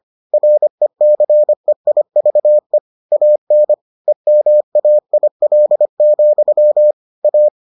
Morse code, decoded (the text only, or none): RECEIVE AN EMAIL, A